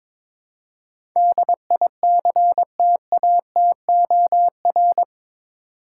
Morse code, heard as a keyed tone, about 22 wpm, 700 Hz. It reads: DICTATOR
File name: practice / 22wpm / Eight Letter Words - 22wpm